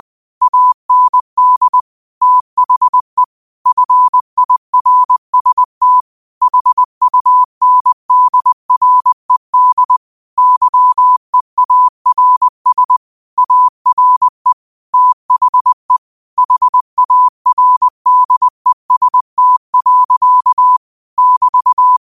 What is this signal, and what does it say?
Morse code, heard as a keyed tone, about 20 wpm, 1000 Hz. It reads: AND THE FIRST HUNDRED YEARS ARE THE HARDEST. <BT>
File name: qotdfiles / 20wpm